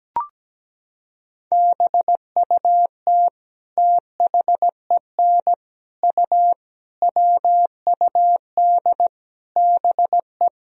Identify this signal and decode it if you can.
Morse code, heard as a keyed tone, about 17 wpm, 700 Hz. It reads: BUT THEN U WUD BE
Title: Sentences from Top 300 Words and Abbreviations 17wpm